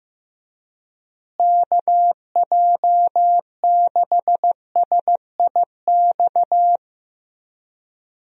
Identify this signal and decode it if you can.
Morse code, heard as a keyed tone, about 15 wpm, 700 Hz. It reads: KJ6SIX
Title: USA Call Signs - 6 Characters - Repeated 2x - 15wpm